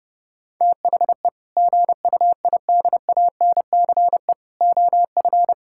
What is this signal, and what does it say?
Morse code, heard as a keyed tone, about 32 wpm, 700 Hz. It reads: THE GUIDANCE OF
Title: Sentences from Top 1600 Words 30wpm